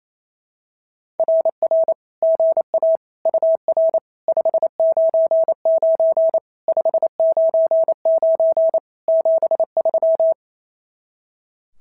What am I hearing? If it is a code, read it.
Morse code, heard as a keyed tone, about 28 wpm, 660 Hz. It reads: RR GA UR 599 599 73
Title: SOTA QSO Practice 28wpm